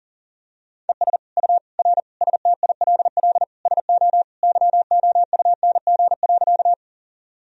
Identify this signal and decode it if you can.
Morse code, heard as a keyed tone, about 40 wpm, 700 Hz. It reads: ES U R STILL SO YOUNG.
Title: Sentences from Top 400 Words and Abbreviations 40wpm